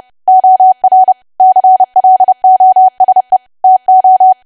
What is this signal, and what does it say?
Morse code, heard as a keyed tone, about 30 wpm, 750 Hz. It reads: OR CLOSE TO
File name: cwa / W1AW-Level3 / 30WPM